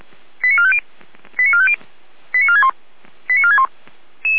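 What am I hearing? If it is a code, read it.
Five-tone selective call (ZVEI), heard as quick runs of five tones, tones of about 70 ms. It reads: 89450 89450 89461 89461
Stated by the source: ZVEI five-tone selective call 88074, received on SDR.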